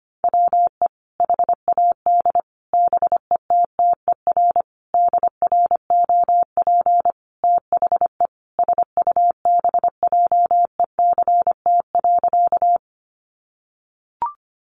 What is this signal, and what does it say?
Morse code, heard as a keyed tone, about 25 wpm, 700 Hz. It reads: WE HAD BETTER DROP THE SUBJECT.
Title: Sentences from Top 1400 Words 25wpm